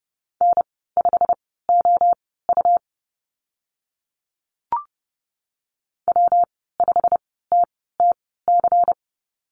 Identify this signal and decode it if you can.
Morse code, heard as a keyed tone, about 32 wpm, spaced as for 17 wpm, 700 Hz. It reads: N5OU W5TTC
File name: Lesson 19g - Letter w - Call Signs 30wpm